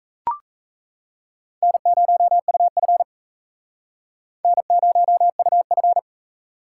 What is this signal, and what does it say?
Morse code, heard as a keyed tone, about 38 wpm, 700 Hz. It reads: N0UF N0UF